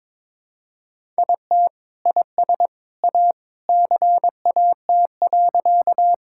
Morse code, heard as a keyed tone, about 22 wpm, 700 Hz. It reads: IT IS A CAT.
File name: practice / 22wpm / Sentences with 2-Word Phrases - 22wpm